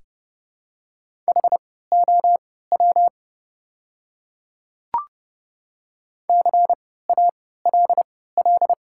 Morse code, heard as a keyed tone, about 32 wpm, spaced as for 17 wpm, 700 Hz. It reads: HOW CALL
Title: 3x Character Spacing - Top 100 Words - 30wpm